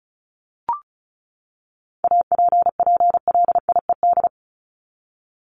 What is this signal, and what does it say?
Morse code, heard as a keyed tone, about 34 wpm, 700 Hz. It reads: APPLIED
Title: ED Suffix - Top 100 Words with Base Words 35wpm